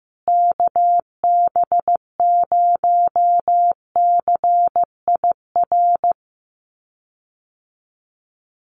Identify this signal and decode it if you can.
Morse code, heard as a keyed tone, about 15 wpm, 700 Hz. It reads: KB0CIR
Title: USA Call Signs Repeated 3x 15wpm